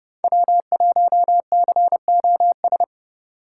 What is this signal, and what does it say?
Morse code, heard as a keyed tone, about 31 wpm, 700 Hz. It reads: W1COS
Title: USA Call Signs - 5 Characters - 3x Character Spacing - 30wpm